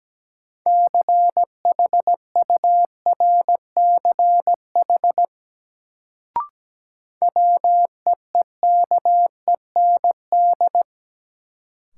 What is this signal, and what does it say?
Morse code, heard as a keyed tone, about 17 wpm, 700 Hz. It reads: CHURCH WEEKEND
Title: Sets of 5 Words - New Word Drill - 17wpm